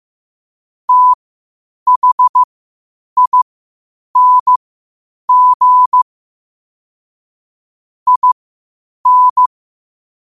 Morse code, heard as a keyed tone, about 15 wpm, spaced as for 8 wpm, 1000 Hz. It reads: THING IN
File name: qotdfiles / 8wpm